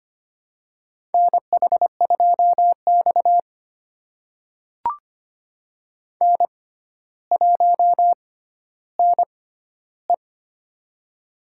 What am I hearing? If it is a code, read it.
Morse code, heard as a keyed tone, about 25 wpm, spaced as for 13 wpm, 700 Hz. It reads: NH2X N1NE
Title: USA Call Signs - 4 Characters - 6x Character Spacing - 25wpm